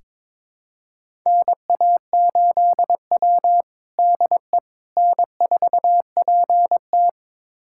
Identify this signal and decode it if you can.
Morse code, heard as a keyed tone, about 22 wpm, 700 Hz. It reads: NA8W DE N4PT
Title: USA Callsign DE Callsign - 4 Characters - 22wpm